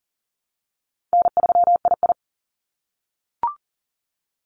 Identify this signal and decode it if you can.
Morse code, heard as a keyed tone, about 38 wpm, 700 Hz. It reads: N3II
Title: NA SOTA Activator Callsigns 40wpm